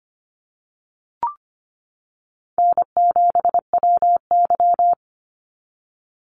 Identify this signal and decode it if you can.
Morse code, heard as a keyed tone, about 25 wpm, 700 Hz. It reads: N7WY
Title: Easy Call Signs - 25wpm